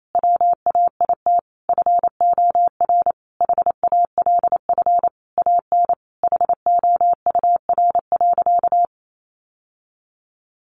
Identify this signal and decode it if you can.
Morse code, heard as a keyed tone, about 28 wpm, 700 Hz. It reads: WAIT FOR HALF AN HOUR.
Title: Sentences from Top 1100 Words 28wpm